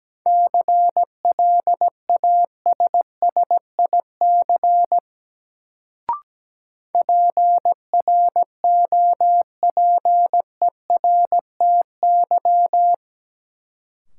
Morse code, heard as a keyed tone, about 17 wpm, 700 Hz. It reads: CLASSIC PROPERTY
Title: Sentences from Top 1000 Words - New Word Drill - 17wpm